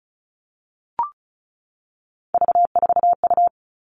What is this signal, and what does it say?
Morse code, heard as a keyed tone, about 34 wpm, 700 Hz. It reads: V4U
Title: All Cut Numbers - 3-Digit 35wpm